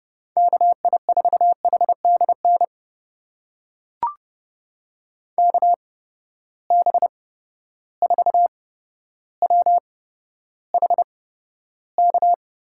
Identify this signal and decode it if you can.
Morse code, heard as a keyed tone, about 30 wpm, spaced as for 13 wpm, 700 Hz. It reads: KI4HDN KB4WHK
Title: USA Call Signs - 6 Characters - 8x Character Spacing - 30wpm